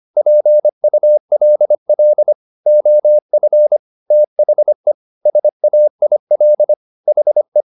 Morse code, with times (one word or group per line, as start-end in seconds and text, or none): 0.2-2.3 PULL
2.7-3.8 OF
4.1-4.9 THE
5.3-6.7 SAIL
7.1-7.6 HE